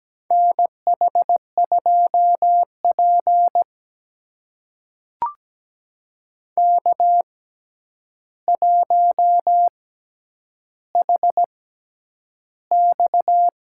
Morse code, heard as a keyed tone, about 17 wpm, spaced as for 9 wpm, 700 Hz. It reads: NH2P K1HX
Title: USA Call Signs - 4 Characters - 6x Character Spacing - 17wpm